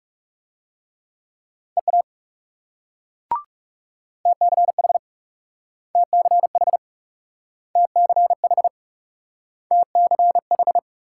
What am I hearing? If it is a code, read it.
Morse code, heard as a keyed tone, about 37 wpm, 700 Hz. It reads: EA TCH TCH TCH TCH
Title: Speed-Racing Multi-letter Phonemes - 30wpm